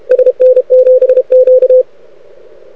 Morse code, heard as a keyed tone, about 30 wpm, 500 Hz, with radio noise. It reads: SN7Q
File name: rec contest / arrl cw 2018 / test 32